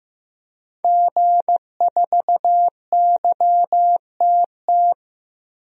Morse code, heard as a keyed tone, about 15 wpm, 700 Hz. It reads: G4YTT
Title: DXCC-100 Call Signs - 5 Characters - 15wpm